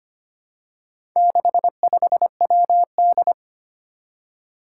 Morse code, heard as a keyed tone, about 25 wpm, 700 Hz. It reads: 65WD
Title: Four Challenging Letter-Number Combinations 25wpm